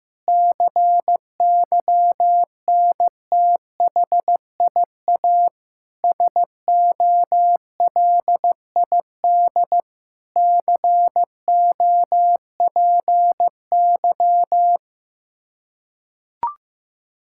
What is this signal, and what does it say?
Morse code, heard as a keyed tone, about 15 wpm, 700 Hz. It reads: CYNTHIA SOLID COPY
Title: Practice QSO - Exchange 3 - More Info - 15wpm